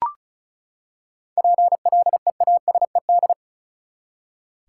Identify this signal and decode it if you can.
Morse code, heard as a keyed tone, about 34 wpm, 700 Hz. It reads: PLEASED
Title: Seven Letter Words - 35wpm